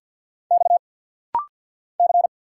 Morse code, heard as a keyed tone, about 50 wpm, 690 Hz. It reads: X C
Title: Single Letter-Number Alt - Rapid-Fire 50wpm